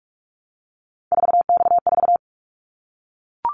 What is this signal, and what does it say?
Morse code, heard as a keyed tone, about 44 wpm, 700 Hz. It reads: 4X4
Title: Callsign Prefixes for Top 50 DXCC - 45wpm